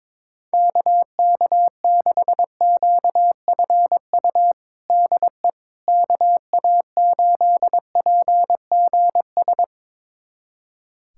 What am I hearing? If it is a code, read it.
Morse code, heard as a keyed tone, about 22 wpm, 700 Hz. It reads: KK6QFU DE KA8PGS
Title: USA Callsign DE Callsign - 6 Characters - 22wpm